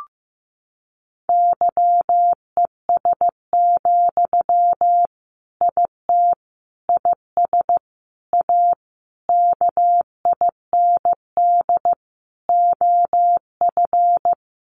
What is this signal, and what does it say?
Morse code, heard as a keyed tone, about 15 wpm, 700 Hz. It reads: YES, IT IS A KIND OF
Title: Sentences from Top 1400 Words 15wpm